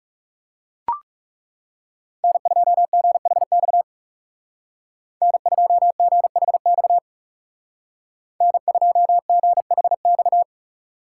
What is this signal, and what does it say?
Morse code, heard as a keyed tone, about 39 wpm, 690 Hz. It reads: N2GHX N2GHX N2GHX